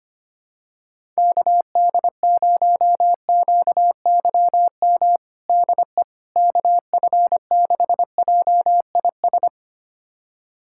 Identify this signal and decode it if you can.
Morse code, heard as a keyed tone, about 25 wpm, 700 Hz. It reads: KD0QYM DE KF6JIS
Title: USA Callsign DE Callsign - 6 Characters - 25wpm